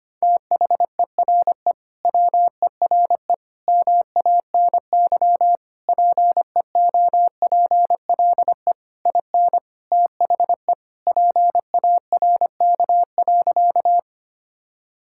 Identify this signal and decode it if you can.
Morse code, heard as a keyed tone, about 25 wpm, 700 Hz. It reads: THERE WERE MANY PEOPLE IN THE PARK.
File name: practice / 25wpm / Sentences with 3-Word Phrases - 25wpm